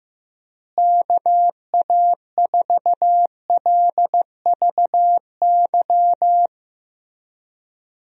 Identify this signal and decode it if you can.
Morse code, heard as a keyed tone, about 15 wpm, 700 Hz. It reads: KA4LVY
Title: USA Call Signs - 6 Characters - Repeated 2x - 15wpm